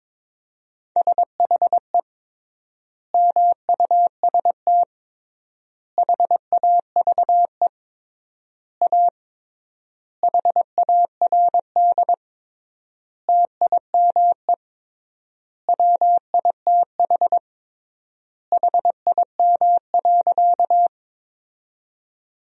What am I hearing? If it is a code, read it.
Morse code, heard as a keyed tone, about 22 wpm, 700 Hz. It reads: SHE MUST HAVE A HARD TIME WITH HIM.
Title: 3x Word-Spacing - Sentences from Top 300 Words - 22wpm